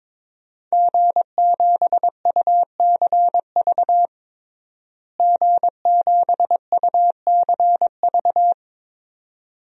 Morse code, heard as a keyed tone, about 22 wpm, 700 Hz. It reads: G7UCV G7UCV
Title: DXCC-100 Call Signs - 5 Characters - Repeated 2x - 22wpm